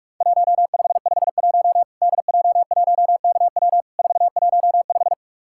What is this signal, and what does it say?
Morse code, heard as a keyed tone, about 44 wpm, 700 Hz. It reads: 1551 DJ1KW 415
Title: Worked All Europe DX Contest - QTC Exchange - 45wpm